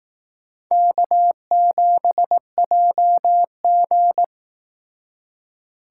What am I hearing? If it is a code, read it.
Morse code, heard as a keyed tone, about 18 wpm, 700 Hz. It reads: K7JG
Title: Speed Racing - USA Call Signs 15wpm